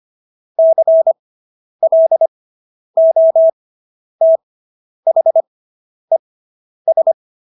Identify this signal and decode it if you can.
Morse code, heard as a keyed tone, about 25 wpm, spaced as for 10 wpm, 650 Hz. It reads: CLOTHES